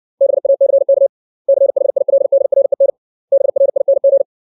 Morse code, heard as a keyed tone, about 60 wpm, 550 Hz. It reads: BACK CHILDREN BRING
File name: 632 words / 632 Words 60WPM